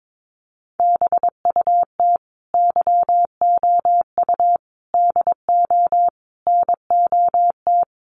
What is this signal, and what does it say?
Morse code, heard as a keyed tone, about 22 wpm, 700 Hz. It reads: BUT YOU DO NOT
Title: Sentences from Top 1100 Words 22wpm